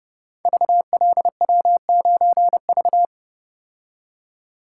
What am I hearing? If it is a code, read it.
Morse code, heard as a keyed tone, about 28 wpm, 700 Hz. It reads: VLW9V
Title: Code Groups - 5 Letter:Numbers - 30wpm